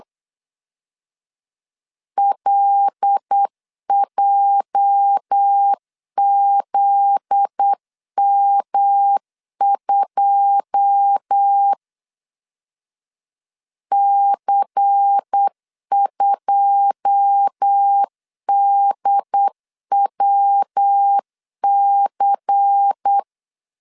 Morse code, heard as a keyed tone, about 8 wpm, 800 Hz. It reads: LJZM2 C2DWC
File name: LEZIONI / 9 2 8wpm 3437